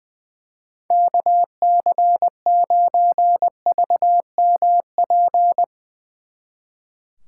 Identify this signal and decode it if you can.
Morse code, heard as a keyed tone, about 20 wpm, 700 Hz. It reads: KC9VMP
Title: Speed Racing - USA 2x3 Call Signs 20wpm